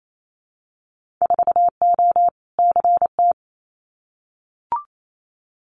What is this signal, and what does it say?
Morse code, heard as a keyed tone, about 28 wpm, 700 Hz. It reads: 4O CT
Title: Winter Field Day Exchanges Encore 28wpm